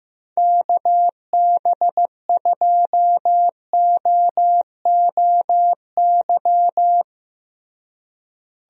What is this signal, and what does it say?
Morse code, heard as a keyed tone, about 15 wpm, 700 Hz. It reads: KB2OOY
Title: USA Call Signs - 6 Characters - Repeated 2x - 15wpm